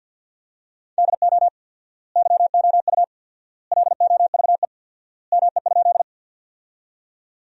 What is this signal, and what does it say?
Morse code, heard as a keyed tone, about 50 wpm, 700 Hz. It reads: DO YOU LOVE ME?
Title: 4x Word-Spacing - Sentences from Top 400 Words - 50wpm